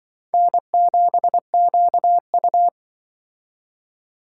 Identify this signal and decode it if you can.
Morse code, heard as a keyed tone, about 24 wpm, 700 Hz. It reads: N7QU
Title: Speed Racing - USA 1x2 Call Signs 20wpm